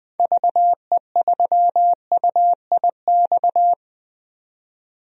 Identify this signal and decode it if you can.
Morse code, heard as a keyed tone, about 20 wpm, 700 Hz. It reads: VE3UIX
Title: DXCC-100 Call Signs - 6 Characters - 20wpm